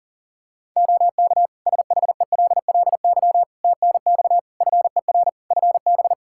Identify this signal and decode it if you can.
Morse code, heard as a keyed tone, about 40 wpm, 700 Hz. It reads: OK SHELLY TNX FER FB
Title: Practice QSO - Exchange 4 - Ending - 40wpm